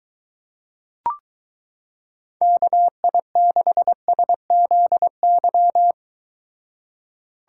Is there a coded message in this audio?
Yes: Morse code from 2.4 to 5.9 s, heard as a keyed tone, about 23 wpm, 700 Hz. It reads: KI6SZY